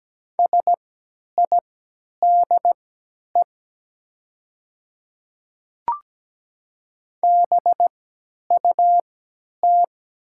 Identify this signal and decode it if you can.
Morse code, heard as a keyed tone, about 17 wpm, spaced as for 10 wpm, 700 Hz. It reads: SIDE BUT
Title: 3x Character Spacing - Top 100 Words - 17wpm